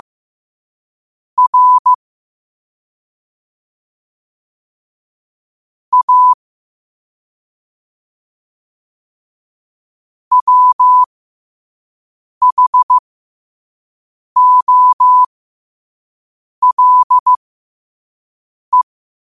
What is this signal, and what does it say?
Morse code, heard as a keyed tone, about 15 wpm, spaced as for 5 wpm, 1000 Hz. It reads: R A WHOLE